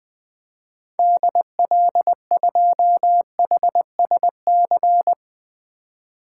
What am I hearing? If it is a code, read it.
Morse code, heard as a keyed tone, about 20 wpm, 700 Hz. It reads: DL2HSC
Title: 2023 SOTA Activator Call Signs - 20wpm